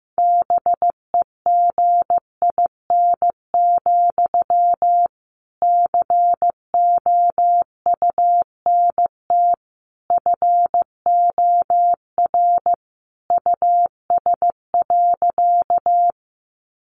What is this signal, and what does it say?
Morse code, heard as a keyed tone, about 15 wpm, 700 Hz. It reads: BEGIN, COUNT FOR US.
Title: Sentences from Top 1500 Words 15wpm